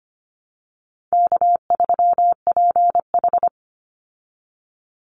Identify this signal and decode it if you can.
Morse code, heard as a keyed tone, about 25 wpm, 700 Hz. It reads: K3PH
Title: Easy Call Signs Repeated 3x - 25wpm